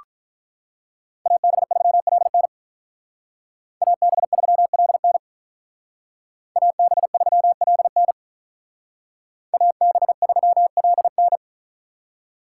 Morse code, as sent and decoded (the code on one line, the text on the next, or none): .- -... ...-- .-.. -. / .- -... ...-- .-.. -. / .- -... ...-- .-.. -. / .- -... ...-- .-.. -.
AB3LN AB3LN AB3LN AB3LN